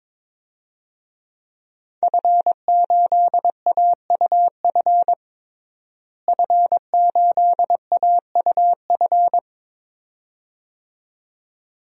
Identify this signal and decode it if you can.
Morse code, heard as a keyed tone, about 22 wpm, 700 Hz. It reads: F8AUF F8AUF
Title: DXCC-100 Call Signs - 5 Characters - Repeated 2x - 22wpm